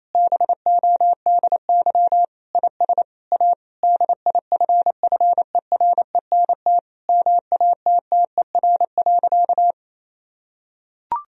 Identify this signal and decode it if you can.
Morse code, heard as a keyed tone, about 28 wpm, 700 Hz. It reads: BODY IS A DIFFERENT MATTER.
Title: Sentences from Top 1000 Words - Review Part 3 - 28wpm